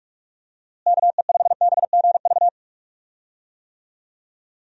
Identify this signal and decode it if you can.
Morse code, heard as a keyed tone, about 46 wpm, 700 Hz. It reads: KE5BGV
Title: USA Call Signs - 6 Characters - Repeated 2x - 45wpm